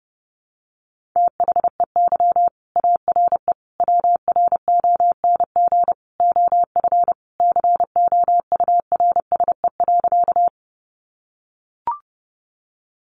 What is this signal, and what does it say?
Morse code, heard as a keyed tone, about 30 wpm, 700 Hz. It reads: THEY ARE WRONG OF COURSE.
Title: Sentences from Top 900 Words 30wpm